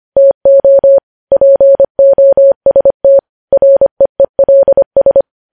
Morse code, heard as a keyed tone, about 25 wpm, 550 Hz. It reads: TO POST REELS
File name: podcast / News headlines-1x-25x25